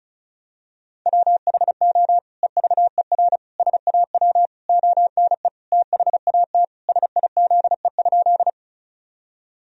Morse code, heard as a keyed tone, about 34 wpm, 700 Hz. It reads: WHO EVER SAW ONE THAT SIZE?
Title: Sentences from Top 1000 Words - Review Part 2 - 35wpm